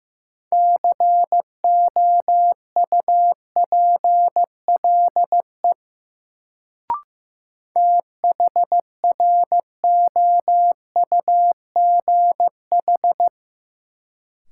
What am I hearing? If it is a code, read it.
Morse code, heard as a keyed tone, about 15 wpm, 700 Hz. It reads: COUPLE THROUGH
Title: Sets of 2 Words - New Word Drill - 15wpm